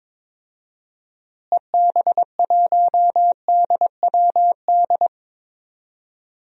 Morse code, heard as a keyed tone, about 22 wpm, 700 Hz. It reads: EB1DWD
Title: DXCC-100 Call Signs - 6 Characters - Repeated 3x - 22wpm